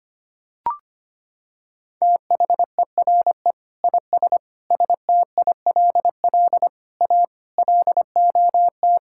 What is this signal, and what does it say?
Morse code, heard as a keyed tone, about 25 wpm, 700 Hz. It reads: THERE IS STILL A LOT
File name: practice / 25wpm / Sentences with 9-Word Phrases - 25wpm